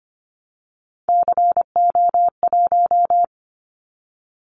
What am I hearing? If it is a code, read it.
Morse code, heard as a keyed tone, about 25 wpm, 700 Hz. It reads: CO1
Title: Callsign Prefixes for Top 100 DXCC - 25wpm